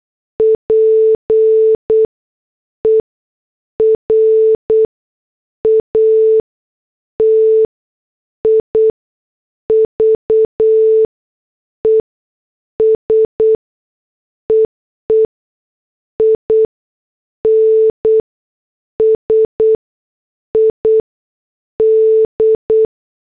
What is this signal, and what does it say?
Morse code, heard as a keyed tone, about 8 wpm, spaced as for 6 wpm, 440 Hz. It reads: PERATIVES EE INSID